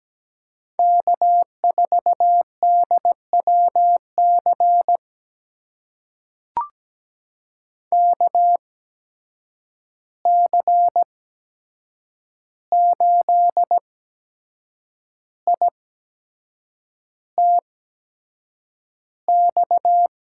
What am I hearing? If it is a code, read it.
Morse code, heard as a keyed tone, about 17 wpm, spaced as for 7 wpm, 700 Hz. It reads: K4DWC KC8ITX
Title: USA Call Signs - 8x Character Spacing - 17wpm